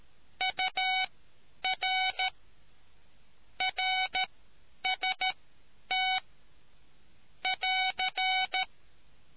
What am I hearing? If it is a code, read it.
Morse code, heard as a keyed tone, about 13 wpm, spaced as for 9 wpm, 750 Hz. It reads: UR RST <AR>